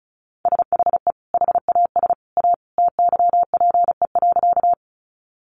Morse code, heard as a keyed tone, about 34 wpm, 700 Hz. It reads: SHE HAS A TYPE.
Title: Sentences from Top 600 Words 35wpm